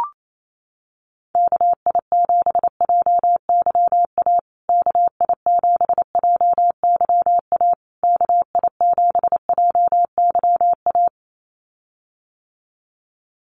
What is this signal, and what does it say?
Morse code, heard as a keyed tone, about 28 wpm, 700 Hz. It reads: KI7JYA KI7JYA KI7JYA
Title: USA Call Signs Repeated 3x 28wpm